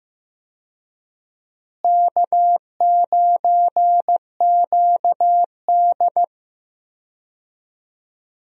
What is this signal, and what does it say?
Morse code, heard as a keyed tone, about 15 wpm, 700 Hz. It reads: K9QD
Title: USA Call Signs - 4 Characters - Repeated 2x - 15wpm